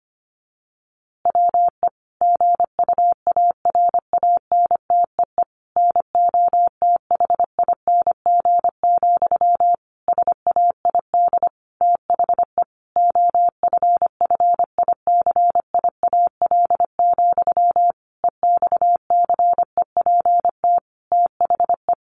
Morse code, heard as a keyed tone, about 25 wpm, 700 Hz. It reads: WE GUARANTEE NOTHING, SAID THE OFFICIAL, EXCEPT THE
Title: A Sound Of Thunder 25wpm